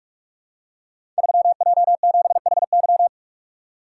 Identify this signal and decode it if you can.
Morse code, heard as a keyed tone, about 44 wpm, 700 Hz. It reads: 3J7HY
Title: Code Groups - 5 Letter:Numbers - 45wpm